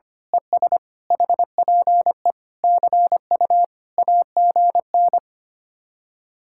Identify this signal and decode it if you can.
Morse code, heard as a keyed tone, about 24 wpm, 700 Hz. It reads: ES HPE CU AGN
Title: Speed-Racing Common QSO Phrases - 25wpm